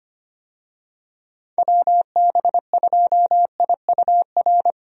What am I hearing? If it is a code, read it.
Morse code, heard as a keyed tone, about 25 wpm, 700 Hz. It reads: WB2IUR
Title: USA Call Signs - 6 Characters - 25wpm